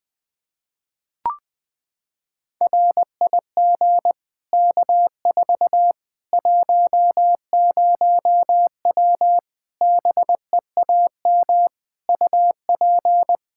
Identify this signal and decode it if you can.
Morse code, heard as a keyed tone, about 20 wpm, 700 Hz. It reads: RIG K4 10W BEAM UP
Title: K1USN SST QSO Practice 20wpm